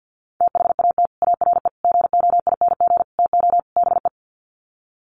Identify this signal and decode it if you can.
Morse code, heard as a keyed tone, about 50 wpm, 700 Hz. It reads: THAT ARE GOING TO BE
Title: Sets of 5 Words Encore2 50wpm